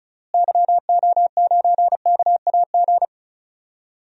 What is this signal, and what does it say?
Morse code, heard as a keyed tone, about 36 wpm, 700 Hz. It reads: YO9KAG
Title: DXCC-100 Call Signs - 6 Characters - Repeated 3x - 35wpm